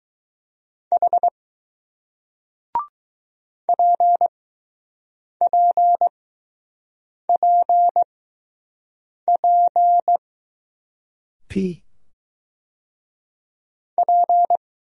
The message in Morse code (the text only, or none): H PPPP P